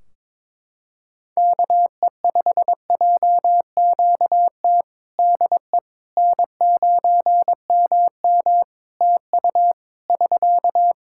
Morse code, heard as a keyed tone, about 22 wpm, 700 Hz. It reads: KE5JQT DE N9MM TU <SK>
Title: K1USN SST QSO Practice 22wpm